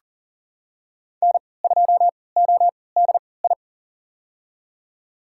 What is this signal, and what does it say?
Morse code, heard as a keyed tone, about 40 wpm, spaced as for 23 wpm, 700 Hz. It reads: N2ODI
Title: USA Call Signs - 5 Characters - 3x Character Spacing - 40wpm